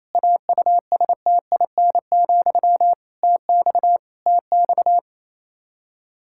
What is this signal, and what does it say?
Morse code, heard as a keyed tone, about 28 wpm, 700 Hz. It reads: AUSTIN, TX TX
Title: K1USN SST QSO Practice 28wpm